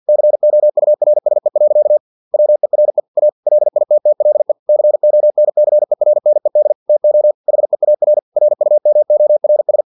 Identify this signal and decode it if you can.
Morse code, heard as a keyed tone, about 50 wpm, 600 Hz. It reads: COURSE. WERE A LITTLE CONCERNED TO HEAR RUMORS
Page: Never 49wpm